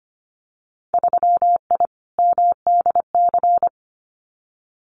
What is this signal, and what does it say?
Morse code, heard as a keyed tone, about 25 wpm, 690 Hz. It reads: 3I MDC
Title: Winter Field Day Exchanges Encore 25wpm